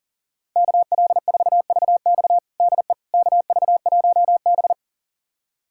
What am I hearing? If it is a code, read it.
Morse code, heard as a keyed tone, about 40 wpm, 700 Hz. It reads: KL4VX DE KV1B
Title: USA Callsign DE Callsign 40wpm